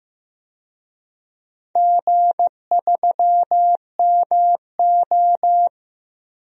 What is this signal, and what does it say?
Morse code, heard as a keyed tone, about 15 wpm, 700 Hz. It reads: G3MO
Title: DXCC-100 Call Signs - 4 Characters - Repeated 3x - 15wpm